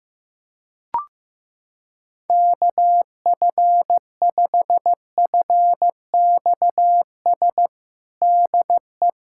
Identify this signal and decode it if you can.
Morse code, heard as a keyed tone, about 16 wpm, 700 Hz. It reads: KF5FXS DE
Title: USA Callsign DE Callsign - 6 Characters - 15wpm